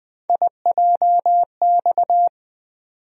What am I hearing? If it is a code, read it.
Morse code, heard as a keyed tone, about 20 wpm, 700 Hz. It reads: IJX
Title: USA Call Sign Postfixes - 3 Letters 20wpm